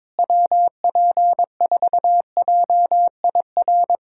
Morse code, heard as a keyed tone, about 22 wpm, 700 Hz. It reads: WP4JIR
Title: USA Callsign DE Callsign 22wpm